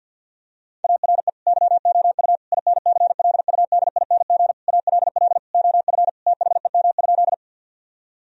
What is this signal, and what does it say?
Morse code, heard as a keyed tone, about 50 wpm, 700 Hz. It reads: ARE YOU INCLUDING ALL OF THEM?
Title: Sentences from Top 1000 Words - Review Part 4 - 50wpm